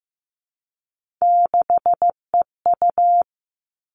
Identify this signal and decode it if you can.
Morse code, heard as a keyed tone, about 15 wpm, 700 Hz. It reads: 6EU
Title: All Cut Numbers - 3-Digit 15wpm